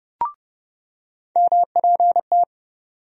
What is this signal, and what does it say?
Morse code, heard as a keyed tone, about 32 wpm, 700 Hz. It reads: MPT
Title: USA Call Sign Postfixes - 3 Letters 30wpm